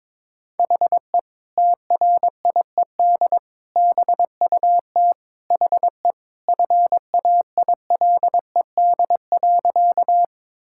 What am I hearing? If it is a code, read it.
Morse code, heard as a keyed tone, about 21 wpm, 700 Hz. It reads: HE TRIED BUT HE FAILED.
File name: practice / 22wpm / Sentences with 2-Word Phrases - 22wpm